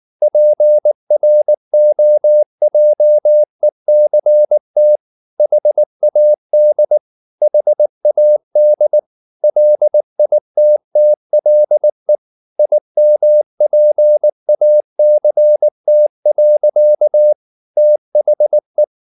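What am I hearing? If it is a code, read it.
Morse code, heard as a keyed tone, about 19 wpm, 600 Hz. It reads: PROJECT HAD HAD LITTLE IMPACT. THE
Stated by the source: Morse code at 19 wpm from CW med Gnister, reading Ken Follett's novel Never in English.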